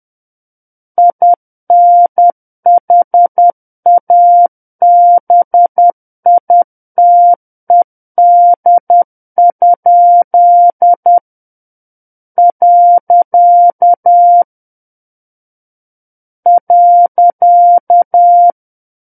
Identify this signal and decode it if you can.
Morse code, heard as a keyed tone, about 10 wpm, 700 Hz. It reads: INHABITED? . .
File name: Book One / 01-Chapter 1 - 10 WPM